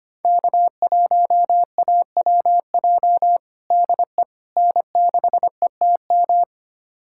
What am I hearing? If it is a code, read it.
Morse code, heard as a keyed tone, about 25 wpm, 700 Hz. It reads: K1AWJ DE N6ETM
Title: USA Callsign DE Callsign 25wpm